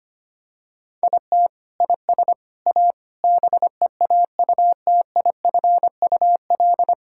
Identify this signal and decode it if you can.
Morse code, heard as a keyed tone, about 25 wpm, 700 Hz. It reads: IT IS A BEAUTIFUL
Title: Sentences from Top 1000 Words - Review Part 6 - 25wpm